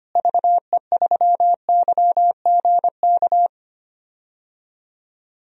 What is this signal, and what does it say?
Morse code, heard as a keyed tone, about 25 wpm, 700 Hz. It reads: VE3YGK
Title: DXCC-100 Call Signs - 6 Characters - Repeated 3x - 25wpm